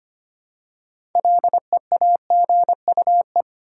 Morse code, heard as a keyed tone, about 25 wpm, 700 Hz. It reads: LEAGUE
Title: Six Letter Words - 25wpm